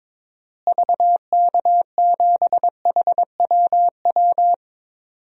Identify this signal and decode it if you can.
Morse code, heard as a keyed tone, about 21 wpm, 700 Hz. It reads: VK7HWW
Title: DXCC-100 Call Signs - 6 Characters - Repeated 3x - 22wpm